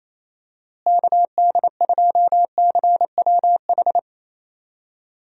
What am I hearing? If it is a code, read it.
Morse code, heard as a keyed tone, about 28 wpm, 700 Hz. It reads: KD2CWH
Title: USA Call Signs - 6 Characters - Repeated 3x - 28wpm